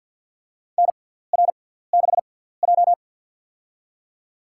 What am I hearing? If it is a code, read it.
Morse code, heard as a keyed tone, about 50 wpm, spaced as for 17 wpm, 690 Hz. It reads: NR6J
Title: USA Call Signs - 4 Characters - 6x Character Spacing - 50wpm